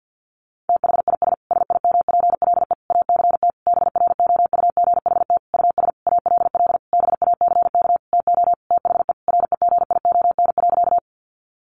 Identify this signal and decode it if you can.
Morse code, heard as a keyed tone, about 50 wpm, 700 Hz. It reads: THIS SIMPLE ACT BROUGHT US ALL BACK TO THE REGION.
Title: Sentences from Top 1500 Words 50wpm